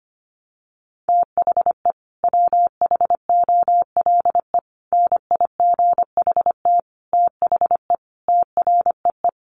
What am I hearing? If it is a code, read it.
Morse code, heard as a keyed tone, about 25 wpm, 700 Hz. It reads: THE WHOLE NIGHT THE TREE
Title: Sentences from Top 500 Words 25wpm